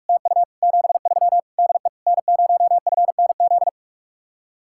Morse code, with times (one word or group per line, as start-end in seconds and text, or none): 0.1-0.4 TU
0.6-1.4 73
1.6-1.9 DE
2.1-3.7 N0FNZ